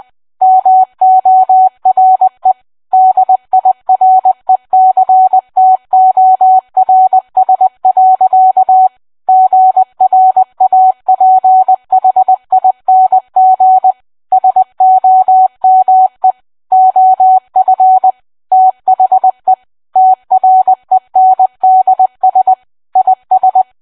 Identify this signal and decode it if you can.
Morse code, heard as a keyed tone, about 20 wpm, 750 Hz. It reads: MORE DIRECTORS. GRAPHING SOME OF THE TRENDS IS